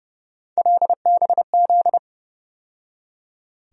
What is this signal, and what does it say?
Morse code, heard as a keyed tone, about 30 wpm, 700 Hz. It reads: LBZ
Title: USA Call Sign Postfixes - 3 Letters 30wpm